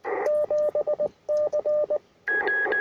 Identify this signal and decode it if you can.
Morse code, heard as a keyed tone, about 19 wpm, 600 Hz, with radio noise. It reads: 7C
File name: attachments / R7AO UR4MCK